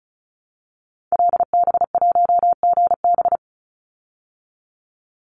Morse code, heard as a keyed tone, about 34 wpm, 700 Hz. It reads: LB1GB
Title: Call Signs Repeated 3x - 35wpm